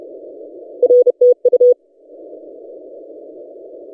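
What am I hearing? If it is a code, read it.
Morse code, heard as a keyed tone, about 32 wpm, 490 Hz. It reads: R TU
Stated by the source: DX station 3D2GC on 20 metres CW, recorded by N6ML.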